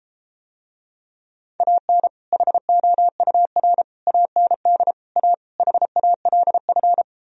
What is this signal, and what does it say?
Morse code, heard as a keyed tone, about 33 wpm, 700 Hz. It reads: AN HOUR AND A HALF